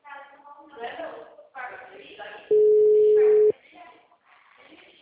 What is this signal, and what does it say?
Ringback tone (Europe, 425 Hz), with speech.
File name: out- -2006- - - .1241